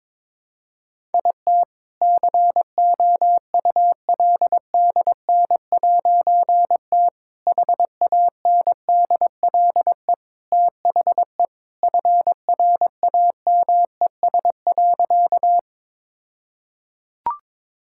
Morse code, heard as a keyed tone, about 22 wpm, 700 Hz. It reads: IT COULDN'T HANDLE THE FRAMES.
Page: morse-code-ninja-practice-22wpm